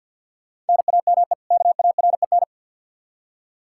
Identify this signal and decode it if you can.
Morse code, heard as a keyed tone, about 52 wpm, 700 Hz. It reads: NAME KAREN
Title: Practice QSO - Exchange 1 - Beginning - 50wpm